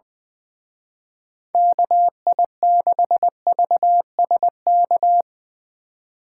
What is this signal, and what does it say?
Morse code, heard as a keyed tone, about 21 wpm, 700 Hz. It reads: KI6VSK
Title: Speed Racing - USA 2x3 Call Signs 20wpm